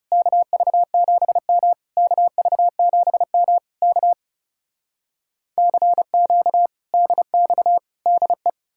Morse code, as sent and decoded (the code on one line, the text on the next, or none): -.- ...- --... -- / -.- ...- --... -- / -.- / -.-. --.- / -.. -..- / -.. .
KV7M KV7M K CQ DX DE